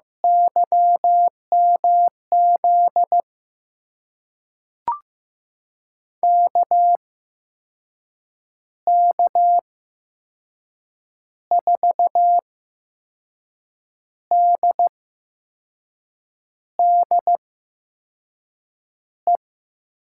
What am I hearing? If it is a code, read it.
Morse code, heard as a keyed tone, about 15 wpm, spaced as for 5 wpm, 700 Hz. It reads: YMZ KK4DDE
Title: USA Call Signs - 6 Characters - 8x Character Spacing - 15wpm